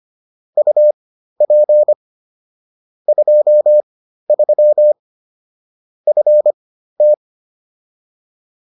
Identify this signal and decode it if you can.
Morse code, heard as a keyed tone, about 25 wpm, spaced as for 13 wpm, 600 Hz. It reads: UP 23 FT